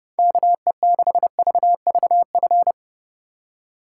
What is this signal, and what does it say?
Morse code, heard as a keyed tone, about 32 wpm, 700 Hz. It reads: KE6VVF